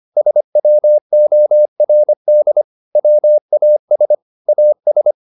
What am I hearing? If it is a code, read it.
Morse code, heard as a keyed tone, about 25 wpm, 600 Hz. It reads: SWORD WAS AS
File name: the old man and the sea part 1 25wpm